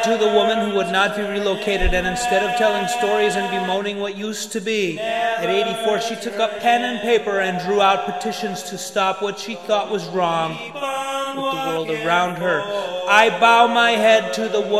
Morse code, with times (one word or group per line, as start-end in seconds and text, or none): none